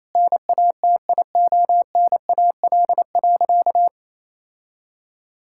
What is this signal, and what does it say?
Morse code, heard as a keyed tone, about 28 wpm, 700 Hz. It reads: NATIONAL.
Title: Sentences from Top 1000 Words - Review Part 5 - 28wpm